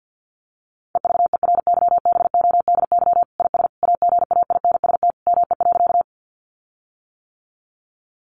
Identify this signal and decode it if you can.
Morse code, heard as a keyed tone, about 50 wpm, 700 Hz. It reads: EVERYBODY IS AGAINST ME.
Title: Sentences from Top 1400 Words 50wpm